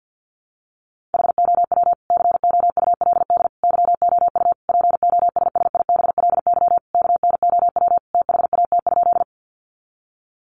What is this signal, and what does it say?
Morse code, heard as a keyed tone, about 50 wpm, 700 Hz. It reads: HOW COULD YOU POSSIBLY KNOW THAT?
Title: Sentences from Top 1300 Words 50wpm